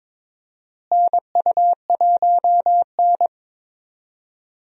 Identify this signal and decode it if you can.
Morse code, heard as a keyed tone, about 21 wpm, 700 Hz. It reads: NU1N
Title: USA Call Signs - 4 Characters - Repeated 3x - 22wpm